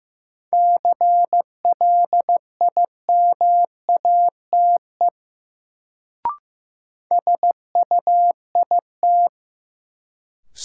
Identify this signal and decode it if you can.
Morse code, heard as a keyed tone, about 15 wpm, 700 Hz. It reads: CLIMATE SUIT
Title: Sentences from Top 1600 Words - New Word Drill Alt - 15wpm